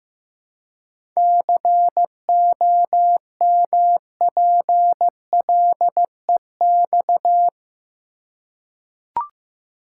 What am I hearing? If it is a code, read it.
Morse code, heard as a keyed tone, about 15 wpm, 700 Hz. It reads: COMPLEX
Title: Seven Letter Words - 15wpm